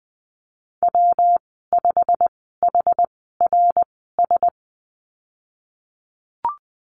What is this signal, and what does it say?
Morse code, heard as a keyed tone, about 20 wpm, spaced as for 14 wpm, 700 Hz. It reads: W5HRS
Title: 179 Lesson 29g - Slash - Call Signs 20wpm